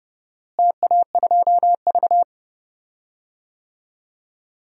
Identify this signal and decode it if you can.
Morse code, heard as a keyed tone, about 30 wpm, 700 Hz. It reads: TA2V
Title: DXCC-100 Call Signs - 4 Characters - Repeated 2x - 30wpm